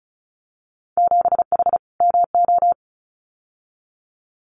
Morse code, heard as a keyed tone, about 36 wpm, 700 Hz. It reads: 7H MO
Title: Winter Field Day Exchanges Encore 35wpm